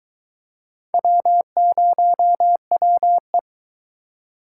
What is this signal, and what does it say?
Morse code, heard as a keyed tone, about 23 wpm, 700 Hz. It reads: W0WE